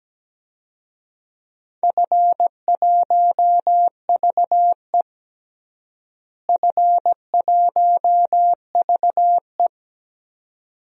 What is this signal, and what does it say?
Morse code, heard as a keyed tone, about 17 wpm, 700 Hz. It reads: F1VE F1VE
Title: DXCC-100 Call Signs - 4 Characters - Repeated 2x - 17wpm